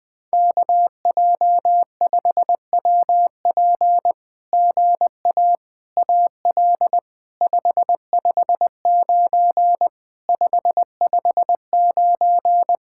Morse code, heard as a keyed tone, about 20 wpm, 700 Hz. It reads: KJ5WP GA AL 559 559
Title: SOTA QSO Practice 20wpm